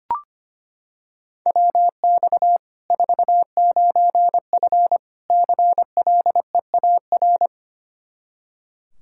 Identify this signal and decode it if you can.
Morse code, heard as a keyed tone, about 25 wpm, 700 Hz. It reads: WX 49F CLEAR
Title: Practice QSO - Exchange 2 - Info - 25wpm